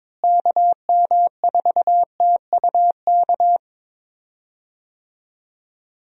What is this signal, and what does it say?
Morse code, heard as a keyed tone, about 22 wpm, 700 Hz. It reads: KM4TUK
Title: USA Call Signs - 6 Characters - Repeated 3x - 22wpm